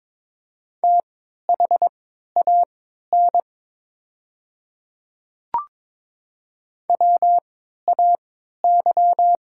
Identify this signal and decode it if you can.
Morse code, heard as a keyed tone, about 22 wpm, spaced as for 12 wpm, 700 Hz. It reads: THAN WAY
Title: 3x Character Spacing - Top 100 Words - 22wpm